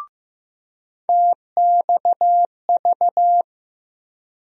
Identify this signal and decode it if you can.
Morse code, heard as a keyed tone, about 15 wpm, 700 Hz. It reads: TXV
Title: USA Call Sign Postfixes - 3 Letters 15wpm